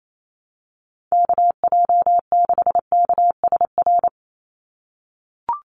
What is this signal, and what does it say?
Morse code, heard as a keyed tone, about 28 wpm, 700 Hz. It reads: KJ6KSR
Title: USA Call Signs 28wpm